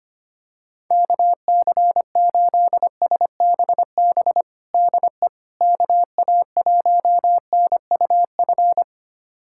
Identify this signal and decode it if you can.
Morse code, heard as a keyed tone, about 25 wpm, 700 Hz. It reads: KC8SBB DE KA1NUF
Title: USA Callsign DE Callsign - 6 Characters - 25wpm